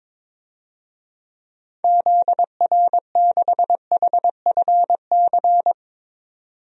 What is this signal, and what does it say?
Morse code, heard as a keyed tone, about 22 wpm, 700 Hz. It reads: ZR6HFC
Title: DXCC-100 Call Signs - 6 Characters - 22wpm